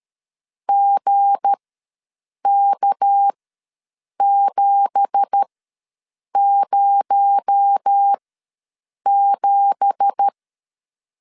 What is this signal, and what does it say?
Morse code, heard as a keyed tone, about 13 wpm, spaced as for 7 wpm, 800 Hz. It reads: GK707
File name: LEZIONI / 9 2867 ws 12wpm 3431